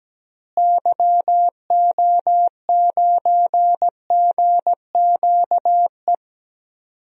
Morse code, heard as a keyed tone, about 17 wpm, 700 Hz. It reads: YO9GQE
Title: DXCC-100 Call Signs - 6 Characters - Repeated 2x - 17wpm